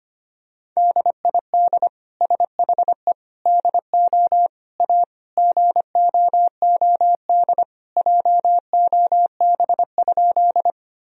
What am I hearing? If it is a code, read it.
Morse code, heard as a keyed tone, about 25 wpm, 700 Hz. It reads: DID SHE DO A GOOD JOB?
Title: Sentences from Top 1000 Words - Review Part 3 - 25wpm